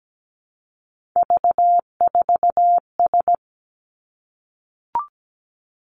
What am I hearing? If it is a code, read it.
Morse code, heard as a keyed tone, about 17 wpm, 700 Hz. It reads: V4S
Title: Three Challenging Letter-Number Combinations 17wpm